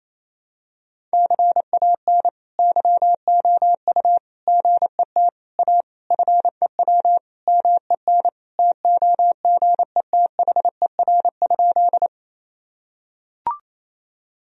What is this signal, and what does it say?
Morse code, heard as a keyed tone, about 27 wpm, 700 Hz. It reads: CAN YOU GET A FEW MEN TOGETHER?
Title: Sentences from Top 1000 Words - Review Part 1 - 28wpm